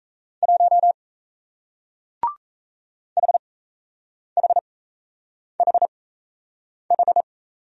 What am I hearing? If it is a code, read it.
Morse code, heard as a keyed tone, about 38 wpm, spaced as for 8 wpm, 700 Hz. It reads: 1 HHHH